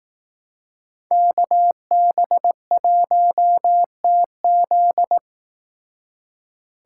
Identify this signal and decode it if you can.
Morse code, heard as a keyed tone, about 18 wpm, 700 Hz. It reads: KB1TZ
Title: Speed Racing - USA 2x2 Call Signs 15wpm